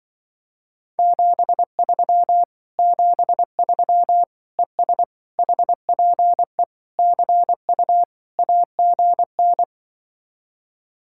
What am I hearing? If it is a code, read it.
Morse code, heard as a keyed tone, about 25 wpm, 700 Hz. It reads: 73 73 ES HPE CU AGN
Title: Speed-Racing Common QSO Phrases - 20wpm